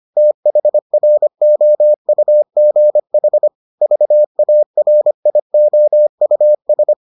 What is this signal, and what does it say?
Morse code, heard as a keyed tone, about 25 wpm, 600 Hz. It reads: THROUGH VARIOUS